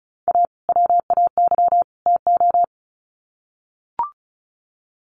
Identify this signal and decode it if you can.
Morse code, heard as a keyed tone, about 34 wpm, 700 Hz. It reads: A WAY TO
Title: Sets of 3 Words 35wpm